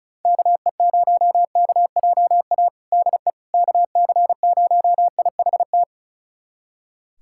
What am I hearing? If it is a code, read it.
Morse code, heard as a keyed tone, about 34 wpm, 700 Hz. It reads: KE0KJA DE KC0IHT
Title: USA Callsign DE Callsign - 6 Characters - 35wpm